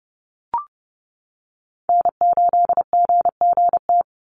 Morse code, heard as a keyed tone, about 30 wpm, 700 Hz. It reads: N8GGT
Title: USA Call Signs 30wpm